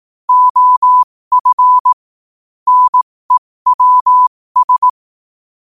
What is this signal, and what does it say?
Morse code, heard as a keyed tone, about 18 wpm, spaced as for 15 wpm, 1000 Hz. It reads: OF NEWS
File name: qotdfiles / 15wpm